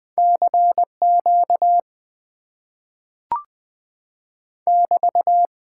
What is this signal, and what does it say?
Morse code, heard as a keyed tone, about 20 wpm, 700 Hz. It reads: CQ <BT>
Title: Top 90 QSO Elements - 8x Character Spacing - 20wpm